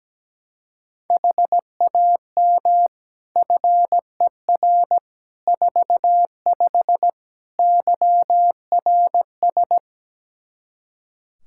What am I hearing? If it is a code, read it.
Morse code, heard as a keyed tone, about 17 wpm, 700 Hz. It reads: HAM FER 45 YRS
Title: K1USN SST QSO Practice 17wpm